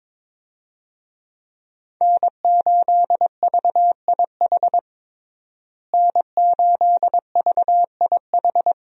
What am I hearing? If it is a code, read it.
Morse code, heard as a keyed tone, about 21 wpm, 700 Hz. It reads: N8VIH N8VIH
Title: USA Call Signs - 5 Characters - Repeated 2x - 22wpm